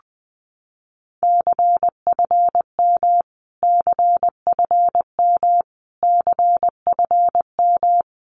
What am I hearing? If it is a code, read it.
Morse code, heard as a keyed tone, about 20 wpm, 700 Hz. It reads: CFM CFM CFM
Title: Abbreviations Repeated 3x - 20wpm